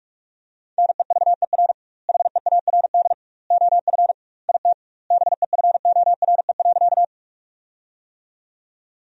Morse code, heard as a keyed tone, about 47 wpm, 700 Hz. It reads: NEVER HEARD OF IT BEFORE.
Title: 2x Word-Spacing - Sentences from Top 400 Words - 45wpm